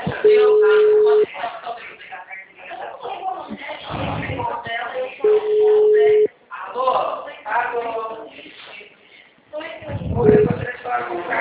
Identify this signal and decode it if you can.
Ringback tone (Europe, 425 Hz), with speech.